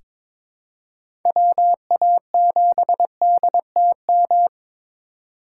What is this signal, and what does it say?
Morse code, heard as a keyed tone, about 22 wpm, 700 Hz. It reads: WA7DTM
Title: USA Call Signs - 6 Characters - 4x Character Spacing - 22wpm